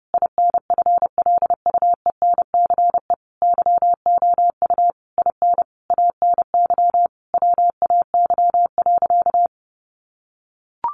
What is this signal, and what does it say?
Morse code, heard as a keyed tone, about 30 wpm, 700 Hz. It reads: INFLUENCE YOU IN ANY WAY.
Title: Sentences from Top 1100 Words 30wpm